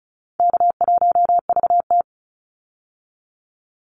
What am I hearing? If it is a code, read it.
Morse code, heard as a keyed tone, about 36 wpm, 700 Hz. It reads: K1VT
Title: Call Signs - 35wpm